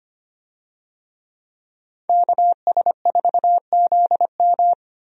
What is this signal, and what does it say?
Morse code, heard as a keyed tone, about 25 wpm, 700 Hz. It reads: KS4ZM
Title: USA Call Signs - 5 Characters - Repeated 2x - 25wpm